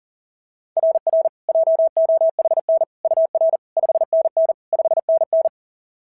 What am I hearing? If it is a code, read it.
Morse code, heard as a keyed tone, about 40 wpm, 690 Hz. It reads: RR JOHN UR 5NN 5NN
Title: SOTA QSO Practice 40wpm